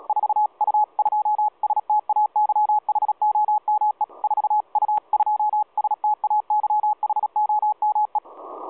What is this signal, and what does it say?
Morse code, heard as a keyed tone, about 36 wpm, 840 Hz. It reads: 4U2STAYHOME 4U2STAYHOME
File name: 4U2STAYHOME 7016.2kHz